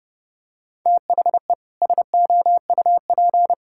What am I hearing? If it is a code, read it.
Morse code, heard as a keyed tone, about 30 wpm, 700 Hz. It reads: THE SOUP
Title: Sentences from Top 1000 Words - Review Part 6 - 30wpm